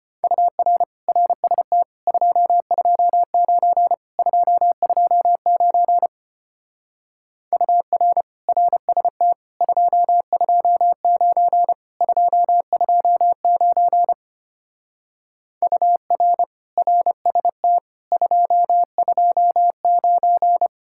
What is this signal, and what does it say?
Morse code, heard as a keyed tone, about 32 wpm, 700 Hz. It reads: UR RST 229 229 UR RST 229 229 UR RST 229